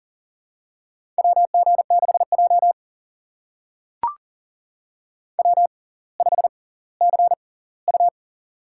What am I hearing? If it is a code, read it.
Morse code, heard as a keyed tone, about 40 wpm, spaced as for 20 wpm, 700 Hz. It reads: WG6J W5CU
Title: USA Call Signs - 4 Characters - 6x Character Spacing - 40wpm